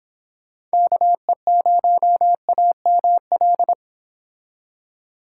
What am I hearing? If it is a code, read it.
Morse code, heard as a keyed tone, about 27 wpm, 700 Hz. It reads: KE0AML
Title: Speed Racing - USA 2x3 Call Signs 22wpm